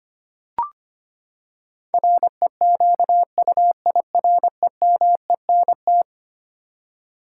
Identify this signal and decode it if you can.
Morse code, heard as a keyed tone, about 25 wpm, 700 Hz. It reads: REQUIREMENT
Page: morse-code-ninja-practice-25wpm